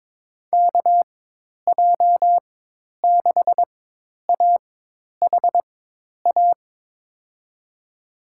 Morse code, heard as a keyed tone, about 22 wpm, spaced as for 10 wpm, 700 Hz. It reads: KJ6AHA
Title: USA Call Signs - 6 Characters - 4x Character Spacing - 22wpm